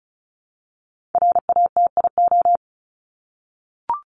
Morse code, heard as a keyed tone, about 34 wpm, 700 Hz. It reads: RATIO
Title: Top 1751-2000 Words 35wpm